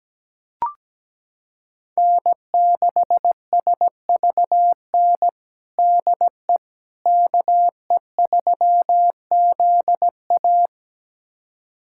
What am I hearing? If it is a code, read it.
Morse code, heard as a keyed tone, about 17 wpm, 700 Hz. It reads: N6SVN DE KE3ZA
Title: USA Callsign DE Callsign - 5 Characters - 17wpm